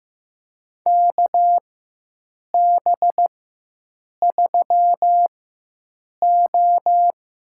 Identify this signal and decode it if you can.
Morse code, heard as a keyed tone, about 15 wpm, spaced as for 7 wpm, 700 Hz. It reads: KB3O
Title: USA Call Signs - 4x Character Spacing - 15wpm